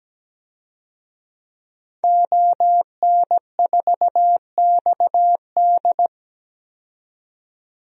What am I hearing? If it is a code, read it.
Morse code, heard as a keyed tone, about 17 wpm, 700 Hz. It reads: ON4XD
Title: DXCC-100 Call Signs - 5 Characters - 17wpm